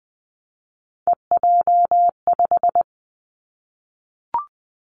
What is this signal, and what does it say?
Morse code, heard as a keyed tone, about 20 wpm, 700 Hz. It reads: EJ5
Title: Three Challenging Letter-Number Combinations 20wpm